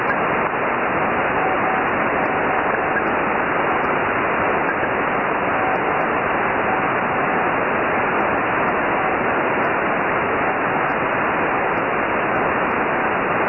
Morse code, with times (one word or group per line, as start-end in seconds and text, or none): none